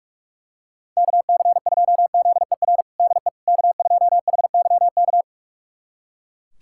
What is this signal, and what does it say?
Morse code, heard as a keyed tone, about 41 wpm, 700 Hz. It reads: KC2ZER DE K2HYK
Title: USA Callsign DE Callsign 45wpm